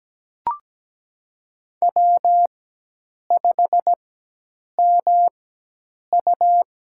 Morse code, heard as a keyed tone, about 17 wpm, spaced as for 8 wpm, 700 Hz. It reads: W5MU
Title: USA Call Signs - 4 Characters - 4x Character Spacing - 17wpm